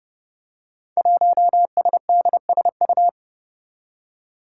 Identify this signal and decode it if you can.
Morse code, heard as a keyed tone, about 30 wpm, 700 Hz. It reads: 1SDSU
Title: Five Challenging Letter-Number Combinations 30wpm